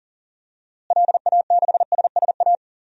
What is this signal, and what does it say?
Morse code, heard as a keyed tone, about 39 wpm, 700 Hz. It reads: LA6SSA
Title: DXCC-100 Call Signs - 6 Characters - Repeated 2x - 40wpm